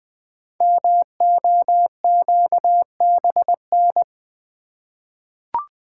Morse code, heard as a keyed tone, about 20 wpm, 700 Hz. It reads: MOQBN
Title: Code Groups - 5 Letters - 20wpm